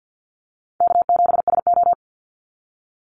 Morse code, heard as a keyed tone, about 50 wpm, 700 Hz. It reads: K7SO
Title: NA SOTA Activator Callsigns 50wpm